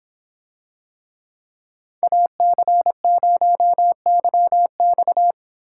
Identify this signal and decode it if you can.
Morse code, heard as a keyed tone, about 26 wpm, 700 Hz. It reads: AC0YX